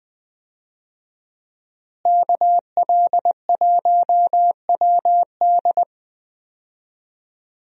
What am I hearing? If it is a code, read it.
Morse code, heard as a keyed tone, about 20 wpm, 700 Hz. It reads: KL1WD
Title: USA Call Signs - 5 Characters - Repeated 2x - 20wpm